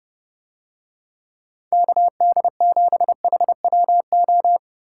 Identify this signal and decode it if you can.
Morse code, heard as a keyed tone, about 30 wpm, 700 Hz. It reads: KD7HWO